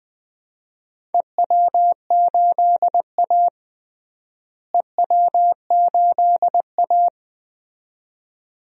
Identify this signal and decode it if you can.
Morse code, heard as a keyed tone, about 20 wpm, 700 Hz. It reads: EW8A EW8A
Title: DXCC-100 Call Signs - 4 Characters - Repeated 2x - 20wpm